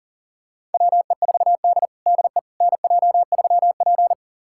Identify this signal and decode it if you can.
Morse code, heard as a keyed tone, about 40 wpm, 700 Hz. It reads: WE4D DE NJ3P
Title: USA Callsign DE Callsign - 4 Characters - 40wpm